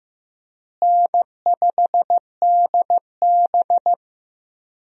Morse code, heard as a keyed tone, about 15 wpm, 700 Hz. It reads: N5DB
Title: USA Call Signs - 4 Characters - Repeated 3x - 15wpm